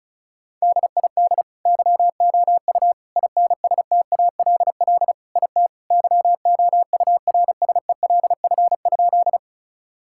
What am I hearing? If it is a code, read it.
Morse code, heard as a keyed tone, about 34 wpm, 700 Hz. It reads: DID YOU INSTALL IT YOURSELF?
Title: Sentences from Top 1600 Words 35wpm